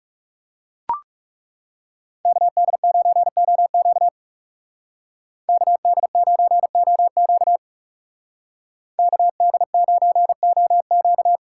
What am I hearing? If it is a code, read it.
Morse code, heard as a keyed tone, about 39 wpm, 700 Hz. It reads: KD9OQ KD9OQ KD9OQ